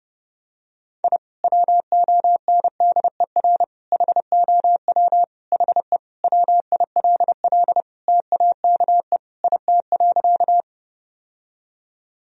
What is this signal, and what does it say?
Morse code, heard as a keyed tone, about 30 wpm, 700 Hz. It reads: I WONDER HOW HE WILL TAKE IT.
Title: Sentences from Top 1000 Words - Review Part 3 - 30wpm